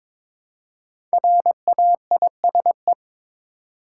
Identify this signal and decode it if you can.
Morse code, heard as a keyed tone, about 22 wpm, 700 Hz. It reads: RAISE
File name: 5 Letter Words - First 3 Letters Given - 22wpm